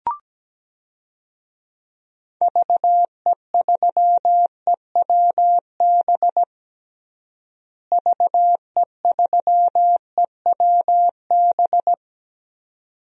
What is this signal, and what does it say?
Morse code, heard as a keyed tone, about 17 wpm, 700 Hz. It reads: VE3EWB VE3EWB
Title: DXCC-100 Call Signs - 6 Characters - Repeated 2x - 17wpm